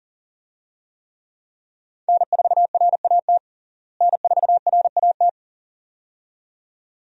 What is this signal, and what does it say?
Morse code, heard as a keyed tone, about 40 wpm, 700 Hz. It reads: N4RAT N4RAT
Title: USA Call Signs - 5 Characters - Repeated 2x - 40wpm